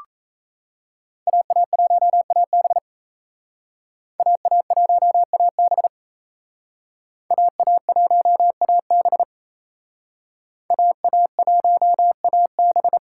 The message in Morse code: AA1AB AA1AB AA1AB AA1AB

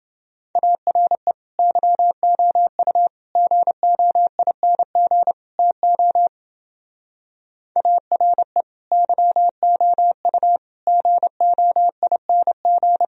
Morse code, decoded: ARE YOU GOING TO ARE YOU GOING

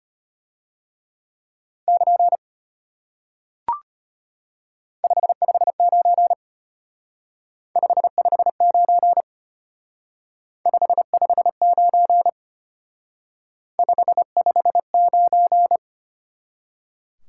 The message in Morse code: <KN> 559 559 559 559

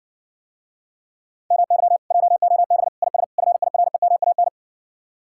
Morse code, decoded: MY JOB IS FIREMAN

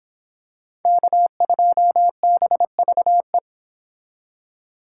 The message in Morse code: K2BVE

